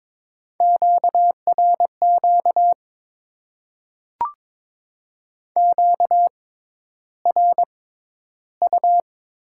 Morse code, heard as a keyed tone, about 21 wpm, spaced as for 11 wpm, 700 Hz. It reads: QRQ QRU
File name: Top 90 QSO Elements - 6x Character Spacing - 22wpm